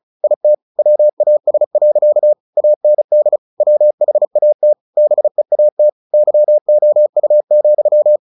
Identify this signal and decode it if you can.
Morse code, heard as a keyed tone, about 34 wpm, 600 Hz. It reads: IT WAS. AND WHAT BEAT YOU,
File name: the old man and the sea 2 35wpm